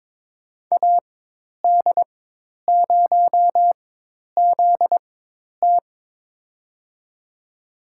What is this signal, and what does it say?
Morse code, heard as a keyed tone, about 22 wpm, spaced as for 10 wpm, 700 Hz. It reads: AD0ZT
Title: USA Call Signs - 5 Characters - 4x Character Spacing - 22wpm